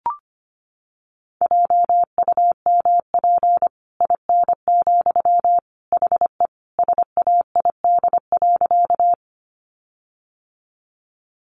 Morse code, heard as a keyed tone, about 25 wpm, 700 Hz. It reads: JUMP IN, HE SAID.
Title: Sentences from Top 1300 Words 25wpm